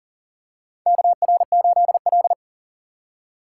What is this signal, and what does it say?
Morse code, heard as a keyed tone, about 40 wpm, 700 Hz. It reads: KR8L
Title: K1USN SST QSO Practice - 40wpm